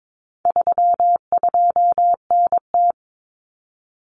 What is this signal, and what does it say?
Morse code, heard as a keyed tone, about 22 wpm, 700 Hz. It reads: 32NT
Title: Common Cut Numbers - 4-Digit 22wpm